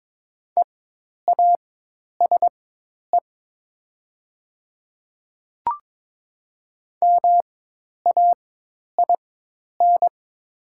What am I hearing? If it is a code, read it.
Morse code, heard as a keyed tone, about 22 wpm, spaced as for 10 wpm, 700 Hz. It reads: EASE MAIN
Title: 4x Character Spacing - Top 500 Words - 22wpm